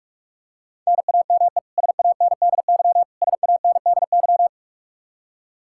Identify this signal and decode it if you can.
Morse code, heard as a keyed tone, about 41 wpm, 700 Hz. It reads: NAME SANDY SANDY
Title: Practice QSO - Exchange 1 - Beginning - 45wpm